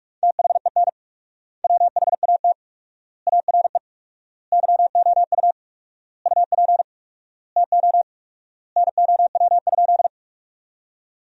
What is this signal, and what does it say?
Morse code, heard as a keyed tone, about 46 wpm, 700 Hz. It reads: THEN WHAT ARE YOU UP TO NOW?
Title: 4x Word-Spacing - Sentences from Top 100 Words - 45wpm